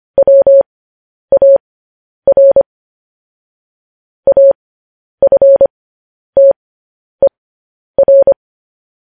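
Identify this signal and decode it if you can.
Morse code, heard as a keyed tone, about 25 wpm, spaced as for 10 wpm, 550 Hz. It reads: WAR AFTER
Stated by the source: Morse code news headlines.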